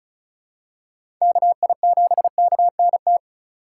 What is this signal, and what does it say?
Morse code, heard as a keyed tone, about 34 wpm, 700 Hz. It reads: KI7KNT
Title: USA Call Signs - 6 Characters - 3x Character Spacing - 35wpm